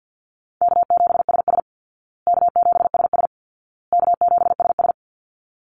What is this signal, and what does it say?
Morse code, heard as a keyed tone, about 50 wpm, 700 Hz. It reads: K7SS K7SS K7SS
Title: Call Signs Repeated 2x - 50wpm